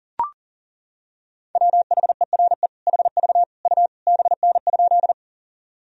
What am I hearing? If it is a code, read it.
Morse code, heard as a keyed tone, about 40 wpm, 700 Hz. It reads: WHERE HV U BN?
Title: Sentences from Top 200 Words and Abbreviations 40wpm